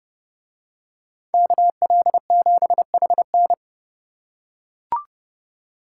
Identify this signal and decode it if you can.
Morse code, heard as a keyed tone, about 28 wpm, 700 Hz. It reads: KL7HN